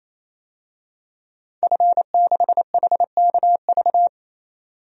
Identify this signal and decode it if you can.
Morse code, heard as a keyed tone, about 28 wpm, 700 Hz. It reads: F6HKV
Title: DXCC-100 Call Signs - 5 Characters - 28wpm